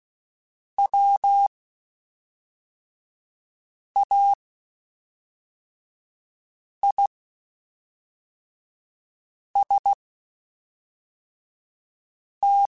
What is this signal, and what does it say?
Morse code, heard as a keyed tone, about 16 wpm, spaced as for 3 wpm, 800 Hz. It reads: WAIST
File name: Spaced / Chuckles Spaced-mp3s / MiddleAge sp @5